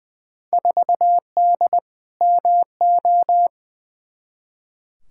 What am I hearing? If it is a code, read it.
Morse code, heard as a keyed tone, about 21 wpm, 700 Hz. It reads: 4D MO
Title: Field Day Exchanges Encore - 20wpm